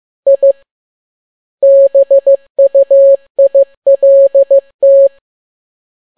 Morse code, heard as a keyed tone, about 15 wpm, 550 Hz. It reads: I BUILT
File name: n6tt p1 15C13